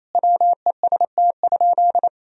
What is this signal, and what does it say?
Morse code, heard as a keyed tone, about 28 wpm, 700 Hz. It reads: WEST?
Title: Sentences from Top 1000 Words - Review Part 2 - 28wpm